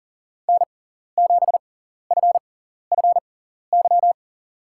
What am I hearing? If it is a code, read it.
Morse code, heard as a keyed tone, about 40 wpm, spaced as for 14 wpm, 700 Hz. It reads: N7FFY
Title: USA Call Signs - 5 Characters - 6x Character Spacing - 40wpm